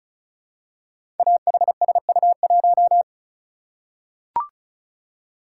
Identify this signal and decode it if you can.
Morse code, heard as a keyed tone, about 36 wpm, 700 Hz. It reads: AHSU1
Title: Code Groups - 5 Letter:Numbers - 35wpm